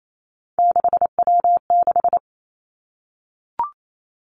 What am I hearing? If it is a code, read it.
Morse code, heard as a keyed tone, about 28 wpm, 700 Hz. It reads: 6W6
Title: Three Challenging Letter-Number Combinations 28wpm